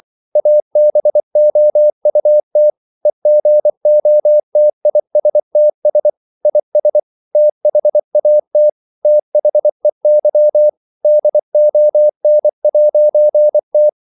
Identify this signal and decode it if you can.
Morse code, heard as a keyed tone, about 24 wpm, 600 Hz. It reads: ABOUT EGOTISTS IS THAT THEY DON'T
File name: CW / Practice / EnglishQuotations0324